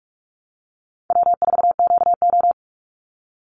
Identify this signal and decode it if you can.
Morse code, heard as a keyed tone, about 48 wpm, 700 Hz. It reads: W4QO
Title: Call Signs Repeated 3x - 45wpm